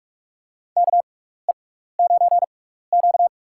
Morse code, heard as a keyed tone, about 46 wpm, spaced as for 16 wpm, 700 Hz. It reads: KE9Q
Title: USA Call Signs - 4 Characters - 6x Character Spacing - 45wpm